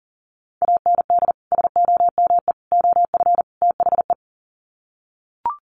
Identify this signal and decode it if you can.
Morse code, heard as a keyed tone, about 40 wpm, 700 Hz. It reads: AND SOME OF THE
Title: Sets of 4 Words 40wpm